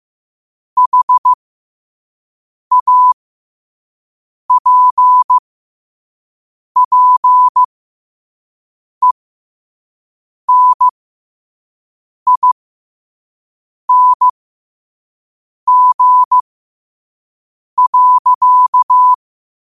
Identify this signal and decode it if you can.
Morse code, heard as a keyed tone, about 15 wpm, spaced as for 5 wpm, 1000 Hz. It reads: HAPPENING.
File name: qotdfiles / 5wpm